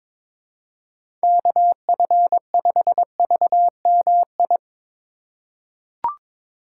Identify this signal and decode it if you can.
Morse code, heard as a keyed tone, about 23 wpm, 700 Hz. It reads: KF5VMI
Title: USA Call Signs - 6 Characters - 4x Character Spacing - 22wpm